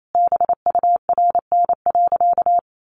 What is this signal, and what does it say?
Morse code, heard as a keyed tone, about 28 wpm, 700 Hz. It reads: BURN.
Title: Sentences from Top 1300 Words 28wpm